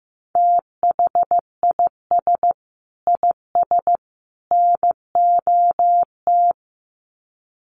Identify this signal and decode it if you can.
Morse code, heard as a keyed tone, about 15 wpm, 700 Hz. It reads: THIS IS NOT
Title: Sets of 3 Words 15wpm